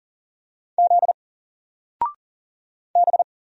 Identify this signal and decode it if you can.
Morse code, heard as a keyed tone, about 40 wpm, 700 Hz. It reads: Z B
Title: Single Character (Letter:Number:Punctuation:Prosign) 40wpm